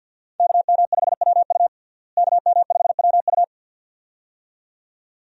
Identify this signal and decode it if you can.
Morse code, heard as a keyed tone, about 53 wpm, 700 Hz. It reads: KM5WU KM5WU
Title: USA Call Signs - 5 Characters - Repeated 2x - 50wpm